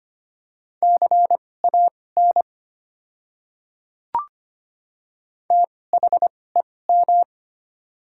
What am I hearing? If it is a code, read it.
Morse code, heard as a keyed tone, about 25 wpm, spaced as for 18 wpm, 690 Hz. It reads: CAN THEM
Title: 2x Character Spacing - Top 100 Words - 25wpm